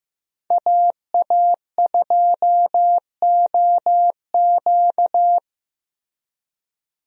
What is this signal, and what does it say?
Morse code, heard as a keyed tone, about 15 wpm, 700 Hz. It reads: AA2OQ
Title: USA Call Signs - 5 Characters - Repeated 2x - 15wpm